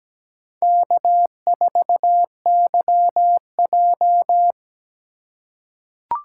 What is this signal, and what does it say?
Morse code, heard as a keyed tone, about 17 wpm, 700 Hz. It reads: K4YJ
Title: USA Call Signs - 4 Characters - 2x Character Spacing - 17wpm